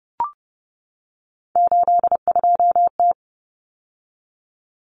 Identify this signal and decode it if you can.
Morse code, heard as a keyed tone, about 30 wpm, 700 Hz. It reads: 82T
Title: Common Cut Numbers - 3-Digit 30wpm